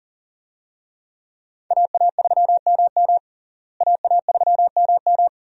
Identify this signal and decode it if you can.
Morse code, heard as a keyed tone, about 40 wpm, 700 Hz. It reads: AA3MM AA3MM
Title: USA Call Signs - 5 Characters - Repeated 3x - 40wpm